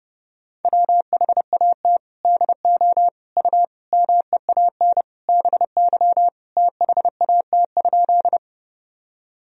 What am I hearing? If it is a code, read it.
Morse code, heard as a keyed tone, about 30 wpm, 700 Hz. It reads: WHAT DO U MEAN BY THAT?
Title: Sentences from Top 200 Words and Abbreviations 30wpm